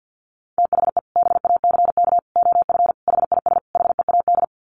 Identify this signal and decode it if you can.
Morse code, heard as a keyed tone, about 50 wpm, 700 Hz. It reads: THE BACK OF HIS HEAD
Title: Sets of 5 Words Encore 50wpm